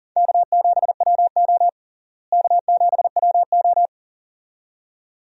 Morse code, heard as a keyed tone, about 40 wpm, 700 Hz. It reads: K7WO K7WO
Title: USA Call Signs - 4 Characters - Repeated 2x - 40wpm